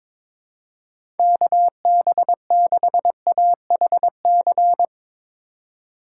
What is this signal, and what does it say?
Morse code, heard as a keyed tone, about 22 wpm, 700 Hz. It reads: KB6AHC
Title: USA Call Signs - 6 Characters - 2x Character Spacing - 22wpm